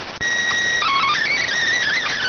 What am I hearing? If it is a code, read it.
SSTV (slow-scan television), PD 290.